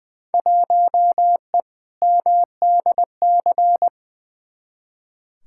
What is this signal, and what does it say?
Morse code, heard as a keyed tone, about 20 wpm, 700 Hz. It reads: 1E MDC
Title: Field Day Exchanges - 20wpm